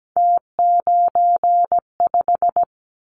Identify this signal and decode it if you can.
Morse code, heard as a keyed tone, about 17 wpm, 700 Hz. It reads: T95
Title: Callsign Prefixes for Top 50 DXCC - 17wpm